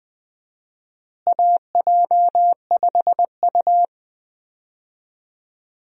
Morse code, heard as a keyed tone, about 20 wpm, 700 Hz. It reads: AJ5U
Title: USA Call Signs - 4 Characters - Repeated 3x - 20wpm